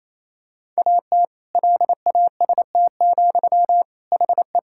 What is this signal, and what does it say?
Morse code, heard as a keyed tone, about 28 wpm, 700 Hz. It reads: AT LAST, HE
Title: Sentences from Top 1000 Words - Review Part 6 - 28wpm